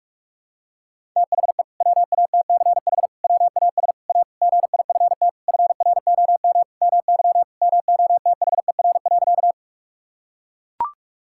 Morse code, heard as a keyed tone, about 41 wpm, 700 Hz. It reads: THE WATCH WAS A GIFT FROM MY MOTHER.
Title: Sentences from Top 1000 Words - Review Part 6 - 45wpm